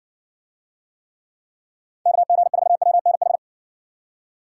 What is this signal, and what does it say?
Morse code, heard as a keyed tone, about 60 wpm, 700 Hz. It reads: KG4WNH